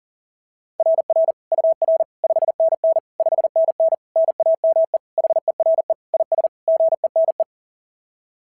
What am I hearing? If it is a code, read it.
Morse code, heard as a keyed tone, about 40 wpm, 660 Hz. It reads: RR UR 5NN 5NN NAME HERE IS GENE
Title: SOTA QSO Practice 40wpm